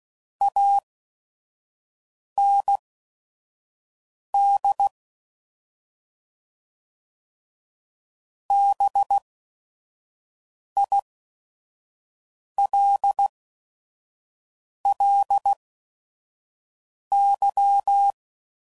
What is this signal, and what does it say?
Morse code, heard as a keyed tone, about 16 wpm, spaced as for 5 wpm, 800 Hz. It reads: AND BILLY